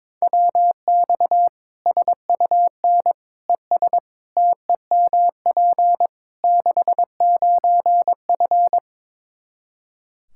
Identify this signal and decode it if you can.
Morse code, heard as a keyed tone, about 22 wpm, 700 Hz. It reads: WX SUN ES TEMP 69F
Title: K1USN SST QSO Practice 22wpm